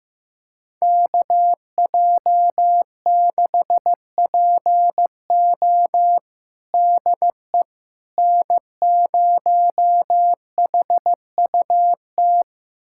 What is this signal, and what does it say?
Morse code, heard as a keyed tone, about 15 wpm, 700 Hz. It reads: KJ6PO DE N0HUT
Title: USA Callsign DE Callsign - 5 Characters - 15wpm